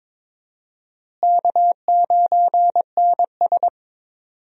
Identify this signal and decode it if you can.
Morse code, heard as a keyed tone, about 22 wpm, 700 Hz. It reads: K9NS
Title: USA Call Signs - 4 Characters - Repeated 3x - 22wpm